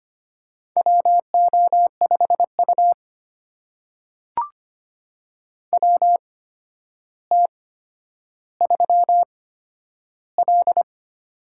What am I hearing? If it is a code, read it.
Morse code, heard as a keyed tone, about 25 wpm, spaced as for 11 wpm, 700 Hz. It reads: WO5U WT3L
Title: USA Call Signs - 4 Characters - 8x Character Spacing - 25wpm